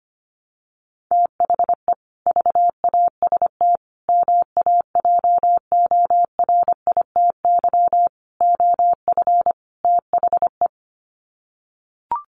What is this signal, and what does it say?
Morse code, heard as a keyed tone, about 25 wpm, 700 Hz. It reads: THE VAST MAJORITY OF THE
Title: Sets of 5 Words Encore 25wpm